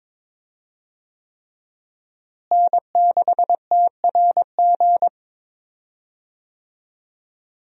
Morse code, heard as a keyed tone, about 22 wpm, 700 Hz. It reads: N6TRG
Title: USA Call Signs - 5 Characters - Repeated 3x - 22wpm